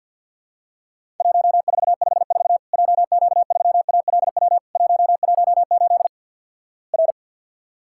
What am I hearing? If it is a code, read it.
Morse code, heard as a keyed tone, about 48 wpm, 700 Hz. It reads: 1454 JQ3ALW 118 R
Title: Worked All Europe DX Contest - QTC Exchange - 50wpm